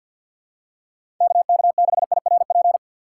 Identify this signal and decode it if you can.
Morse code, heard as a keyed tone, about 47 wpm, 700 Hz. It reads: KK6IRP
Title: USA Call Signs - 6 Characters - 6x Character Spacing - 50wpm